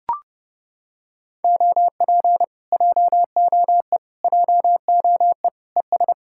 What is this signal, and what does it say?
Morse code, heard as a keyed tone, about 29 wpm, 700 Hz. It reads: OP JOE JOE ES